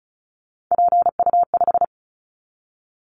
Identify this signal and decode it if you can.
Morse code, heard as a keyed tone, about 34 wpm, 700 Hz. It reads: PU5
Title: Callsign Prefixes for Top 100 DXCC - 35wpm